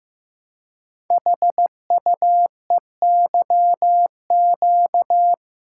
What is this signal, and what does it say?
Morse code, heard as a keyed tone, about 15 wpm, 700 Hz. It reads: HUEYQ
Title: Code Groups - 5 Letters - 15wpm